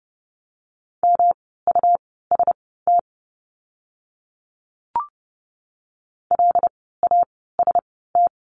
Morse code, heard as a keyed tone, about 30 wpm, spaced as for 17 wpm, 700 Hz. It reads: MUST LAST
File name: course / 30:10wpm / 237 Lesson 38f - Letter q - Words 30wpm